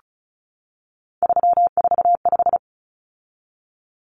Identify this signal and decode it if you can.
Morse code, heard as a keyed tone, about 34 wpm, 700 Hz. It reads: 345
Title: Common Cut Numbers - 3-Digit 35wpm